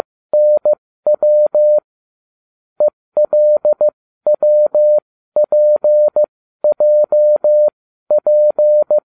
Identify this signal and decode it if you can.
Morse code, heard as a keyed tone, about 15 wpm, spaced as for 13 wpm, 600 Hz. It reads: NW ELWPJP